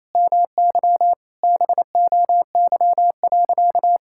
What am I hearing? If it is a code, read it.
Morse code, heard as a keyed tone, about 28 wpm, 690 Hz. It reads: MY BOY.
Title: Sentences from Top 1000 Words - Review Part 6 - 28wpm